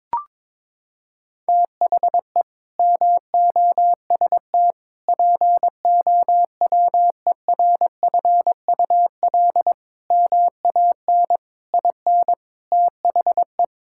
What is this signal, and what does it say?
Morse code, heard as a keyed tone, about 22 wpm, 700 Hz. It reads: THE MOST POWERFUL MAN IN THE
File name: practice / 22wpm / Sets of 7 Words - 22wpm